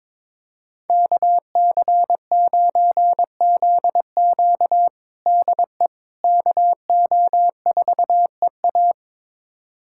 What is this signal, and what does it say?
Morse code, heard as a keyed tone, about 22 wpm, 700 Hz. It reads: KC9ZQ DE KO4EA
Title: USA Callsign DE Callsign - 5 Characters - 22wpm